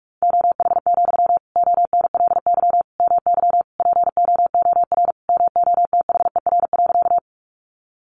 Morse code, heard as a keyed tone, about 44 wpm, 700 Hz. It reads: OH, ONLY MY POOR MOTHER.
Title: Sentences from Top 1000 Words 45wpm